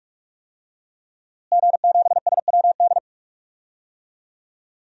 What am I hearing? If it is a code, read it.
Morse code, heard as a keyed tone, about 46 wpm, 700 Hz. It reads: G7SWD
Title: DXCC-100 Call Signs - 5 Characters - Repeated 2x - 45wpm